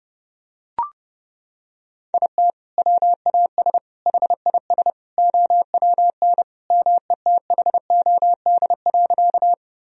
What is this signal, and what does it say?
Morse code, heard as a keyed tone, about 31 wpm, 700 Hz. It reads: IT WAS HIS OWN METHOD.
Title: Sentences from Top 1000 Words - Review Part 3 - 30wpm